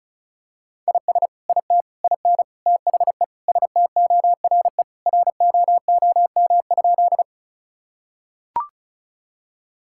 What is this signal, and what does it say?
Morse code, heard as a keyed tone, about 36 wpm, 700 Hz. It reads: IS IT IN THE STORE ROOM?
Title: Sentences from Top 1000 Words - Review Part 5 - 35wpm